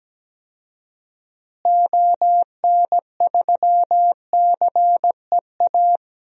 Morse code, heard as a keyed tone, about 17 wpm, 700 Hz. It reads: ON3CEA
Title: DXCC-100 Call Signs - 6 Characters - Repeated 3x - 17wpm